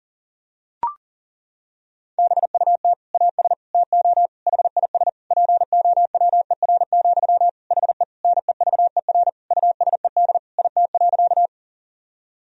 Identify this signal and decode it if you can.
Morse code, heard as a keyed tone, about 37 wpm, 700 Hz. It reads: BUT AS TO HIS POWER, HE NEVER USED IT.
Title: Sentences from Top 1000 Words - Review Part 2 - 40wpm